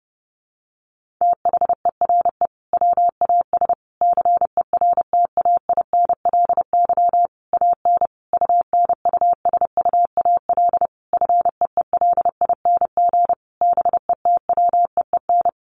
Morse code, heard as a keyed tone, about 30 wpm, 700 Hz. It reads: THERE WAS CERTAINLY AN UNUSUAL FEELING BETWEEN
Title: Sentences from Top 1400 Words 30wpm